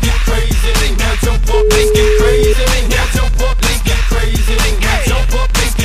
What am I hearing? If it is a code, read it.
Ringback tone (Europe, 425 Hz), with noise.